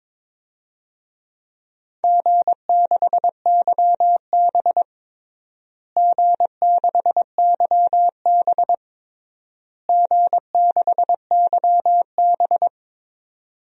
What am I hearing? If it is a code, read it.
Morse code, heard as a keyed tone, about 23 wpm, 700 Hz. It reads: G6YB G6YB G6YB
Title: DXCC-100 Call Signs - 4 Characters - Repeated 3x - 22wpm